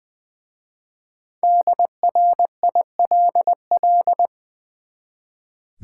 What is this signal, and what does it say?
Morse code, heard as a keyed tone, about 20 wpm, 700 Hz. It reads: DRILL
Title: Easy Words with Hints - 20wpm